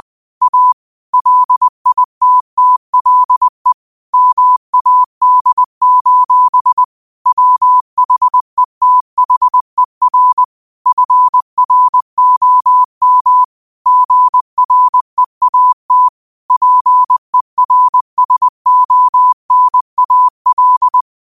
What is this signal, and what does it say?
Morse code, heard as a keyed tone, about 20 wpm, 1000 Hz. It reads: A LITTLE MAD: WHETHER FROM GREAT PERSONAL
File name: qotdfiles / 20wpm